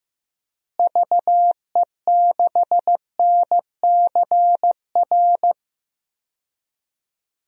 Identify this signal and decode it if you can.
Morse code, heard as a keyed tone, about 15 wpm, 700 Hz. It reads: VE6NCR
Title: DXCC-100 Call Signs - 6 Characters - Repeated 3x - 15wpm